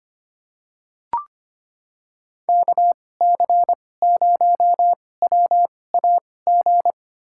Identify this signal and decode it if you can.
Morse code, heard as a keyed tone, about 25 wpm, spaced as for 18 wpm, 700 Hz. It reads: KC0WAG
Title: USA Call Signs - 6 Characters - 2x Character Spacing - 25wpm